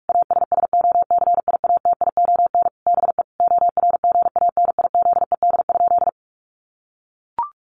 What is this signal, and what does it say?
Morse code, heard as a keyed tone, about 44 wpm, 700 Hz. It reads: ASSOCIATION BE ORGANIZED?
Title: Sentences from Top 1300 Words 45wpm